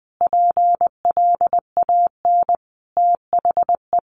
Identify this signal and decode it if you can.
Morse code, heard as a keyed tone, about 20 wpm, 700 Hz. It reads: PLAN THE
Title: Sentences from Top 1400 Words 20wpm